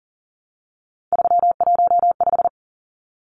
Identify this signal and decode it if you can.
Morse code, heard as a keyed tone, about 40 wpm, 700 Hz. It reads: 315
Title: Numbers - 3-Digit 40wpm